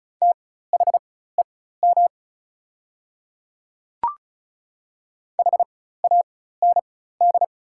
Morse code, heard as a keyed tone, about 36 wpm, spaced as for 16 wpm, 690 Hz. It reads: THEM HAND
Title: 4x Character Spacing - Top 500 Words - 35wpm